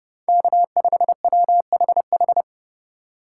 Morse code, heard as a keyed tone, about 29 wpm, 700 Hz. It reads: K5WHH